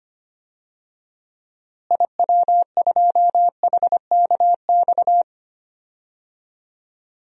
Morse code, heard as a keyed tone, about 25 wpm, 700 Hz. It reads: IW2HKX
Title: DXCC-100 Call Signs - 6 Characters - Repeated 3x - 25wpm